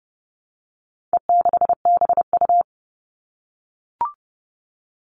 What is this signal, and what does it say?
Morse code, heard as a keyed tone, about 30 wpm, 700 Hz. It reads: E6BU
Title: All Cut Numbers - 4-Digit 30wpm